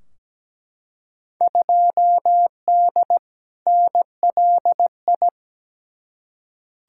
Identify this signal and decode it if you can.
Morse code, heard as a keyed tone, about 17 wpm, 700 Hz. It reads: 2D NLI
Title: Field Day Exchanges - 17wpm